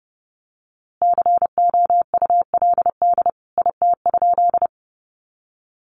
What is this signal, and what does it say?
Morse code, heard as a keyed tone, about 30 wpm, 700 Hz. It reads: COULD IT?
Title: A Sound Of Thunder 30wpm